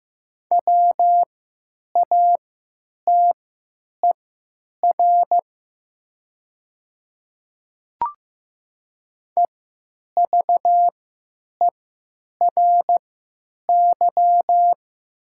Morse code, heard as a keyed tone, about 15 wpm, spaced as for 9 wpm, 700 Hz. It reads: WATER EVERY
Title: 3x Character Spacing - Top 500 Words - 15wpm